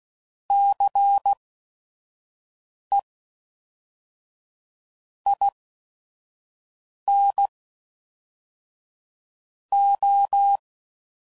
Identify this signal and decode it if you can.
Morse code, heard as a keyed tone, about 16 wpm, spaced as for 5 wpm, 800 Hz. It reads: CE IN O
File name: CW Code Groups @5WPM / Technical Subjects / XC